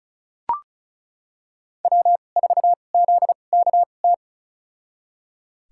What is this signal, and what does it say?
Morse code, heard as a keyed tone, about 36 wpm, spaced as for 25 wpm, 700 Hz. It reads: W4ZKT
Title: USA Call Signs - 5 Characters - 2x Character Spacing - 35wpm